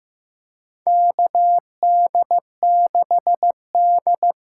Morse code, heard as a keyed tone, about 15 wpm, 700 Hz. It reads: KD6D
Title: USA Call Signs - 6 Characters - 8x Character Spacing - 15wpm